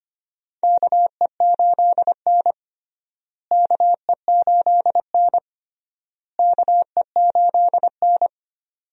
Morse code, heard as a keyed tone, about 25 wpm, 700 Hz. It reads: KE8N KE8N KE8N
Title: USA Call Signs - 4 Characters - Repeated 3x - 25wpm